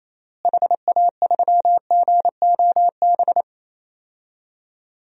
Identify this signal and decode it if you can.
Morse code, heard as a keyed tone, about 28 wpm, 700 Hz. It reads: HA3GOB
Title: DXCC-100 Call Signs - 6 Characters - Repeated 3x - 28wpm